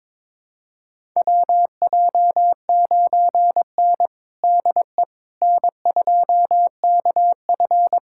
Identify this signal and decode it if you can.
Morse code, heard as a keyed tone, about 21 wpm, 700 Hz. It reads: WJ9N DE N2KF
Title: USA Callsign DE Callsign - 4 Characters - 22wpm